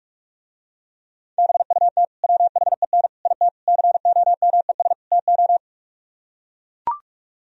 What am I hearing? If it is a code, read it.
Morse code, heard as a keyed tone, about 44 wpm, 700 Hz. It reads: BUT WHEN IT COMES TO